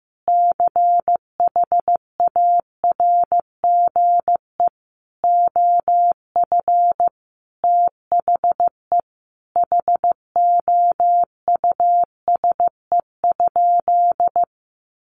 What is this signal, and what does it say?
Morse code, heard as a keyed tone, about 15 wpm, 700 Hz. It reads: CHARGE OF THE HOUSE?
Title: Sentences from Top 900 Words 15wpm